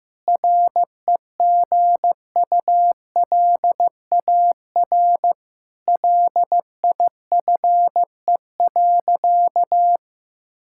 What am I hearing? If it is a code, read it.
Morse code, heard as a keyed tone, about 14 wpm, 700 Hz. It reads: REGULAR LIFE.
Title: Sentences from Top 1000 Words - Review Part 6 - 15wpm